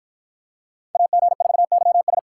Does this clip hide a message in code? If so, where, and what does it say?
Yes: Morse code from 1.0 to 2.2 s, heard as a keyed tone, about 51 wpm, 700 Hz. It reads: AG4YS